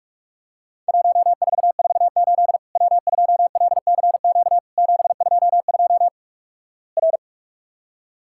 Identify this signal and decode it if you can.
Morse code, heard as a keyed tone, about 47 wpm, 700 Hz. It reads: 1448 W2LCQ 722 R
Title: Worked All Europe DX Contest - QTC Exchange - 45wpm